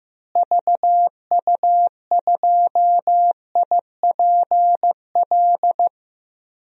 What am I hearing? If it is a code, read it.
Morse code, heard as a keyed tone, about 15 wpm, 700 Hz. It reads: VU2IPL
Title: DXCC-100 Call Signs - 6 Characters - Repeated 3x - 15wpm